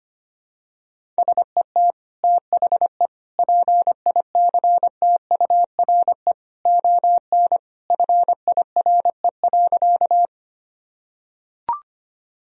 Morse code, heard as a keyed tone, about 25 wpm, 700 Hz. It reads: SET THE PICTURE ON FIRE.
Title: Sentences from Top 1000 Words - Review Part 1 - 25wpm